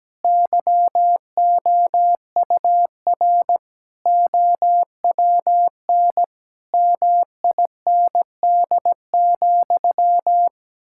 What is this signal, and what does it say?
Morse code, heard as a keyed tone, about 17 wpm, 700 Hz. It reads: YOUR OWN MIND,
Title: Sentences from Top 1000 Words - Review Part 6 - 17wpm